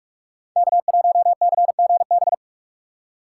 Morse code, heard as a keyed tone, about 46 wpm, 700 Hz. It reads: K1CGB
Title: Code Groups - 5 Letter:Numbers - 45wpm